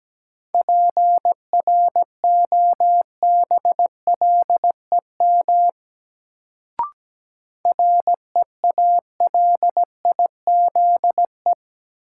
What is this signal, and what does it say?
Morse code, heard as a keyed tone, about 17 wpm, 700 Hz. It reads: PROBLEM REALIZE
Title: Sentences from Top 1000 Words - New Word Drill - 17wpm